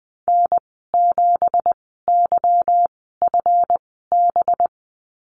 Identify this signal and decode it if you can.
Morse code, heard as a keyed tone, about 20 wpm, spaced as for 14 wpm, 700 Hz. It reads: N7YFB
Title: 193 Lesson 31g - Letter v - Call Signs 20wpm